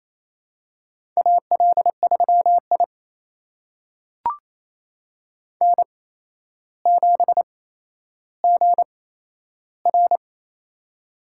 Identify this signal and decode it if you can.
Morse code, heard as a keyed tone, about 28 wpm, spaced as for 12 wpm, 700 Hz. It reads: AL3I N7GR
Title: USA Call Signs - 4 Characters - 8x Character Spacing - 28wpm